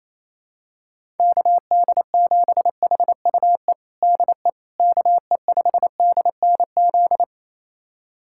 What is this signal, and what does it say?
Morse code, heard as a keyed tone, about 28 wpm, 700 Hz. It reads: KD7HUE DE KE5DNZ
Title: USA Callsign DE Callsign - 6 Characters - 28wpm